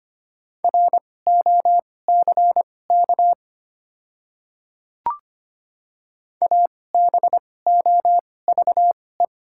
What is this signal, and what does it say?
Morse code, heard as a keyed tone, about 25 wpm, spaced as for 18 wpm, 700 Hz. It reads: ROCK ABOVE
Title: 2x Character Spacing - Top 500 Words - 25wpm